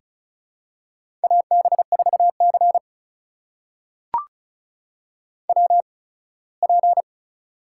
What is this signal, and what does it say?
Morse code, heard as a keyed tone, about 36 wpm, spaced as for 21 wpm, 700 Hz. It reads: AB4C WP
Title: USA Call Signs - 4 Characters - 8x Character Spacing - 35wpm